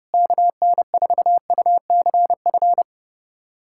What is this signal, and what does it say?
Morse code, heard as a keyed tone, about 30 wpm, 700 Hz. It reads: KN4UCF
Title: USA Callsign DE Callsign 30wpm